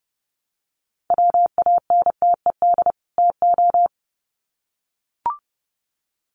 Morse code, heard as a keyed tone, about 30 wpm, 700 Hz. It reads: WANTED TO
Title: Sets of 2 Words 30wpm